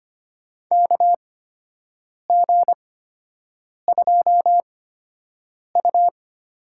Morse code, heard as a keyed tone, about 25 wpm, spaced as for 7 wpm, 700 Hz. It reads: KG2U
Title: USA Call Signs - 8x Character Spacing - 25wpm